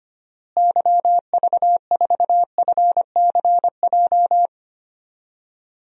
Morse code, heard as a keyed tone, about 25 wpm, 700 Hz. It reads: YV4FCJ
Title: DXCC-100 Call Signs - 6 Characters - Repeated 3x - 25wpm